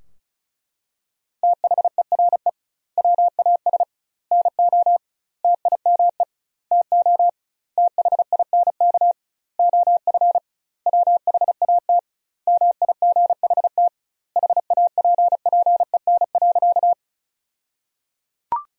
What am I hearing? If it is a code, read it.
Morse code, heard as a keyed tone, about 34 wpm, 700 Hz. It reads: THERE WAS NO TIME TO THINK OF WHAT MIGHT HAPPEN.
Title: 2x Word-Spacing - Sentences from Top 400 Words - 35wpm